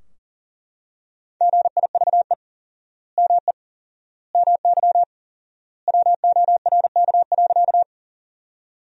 Morse code, heard as a keyed tone, about 40 wpm, 700 Hz. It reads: GIVE ME MY WORK.
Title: 4x Word-Spacing - Sentences from Top 200 Words - 40wpm